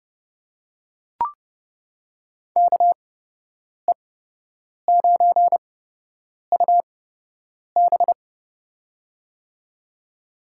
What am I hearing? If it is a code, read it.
Morse code, heard as a keyed tone, about 31 wpm, spaced as for 8 wpm, 700 Hz. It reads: KE9UB
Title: USA Call Signs - 5 Characters - 8x Character Spacing - 30wpm